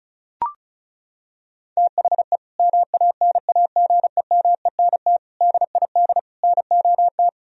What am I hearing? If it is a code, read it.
Morse code, heard as a keyed tone, about 34 wpm, 700 Hz. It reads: THE MANAGEMENT DID NOT
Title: Sentences from Top 1000 Words - Review Part 4 - 35wpm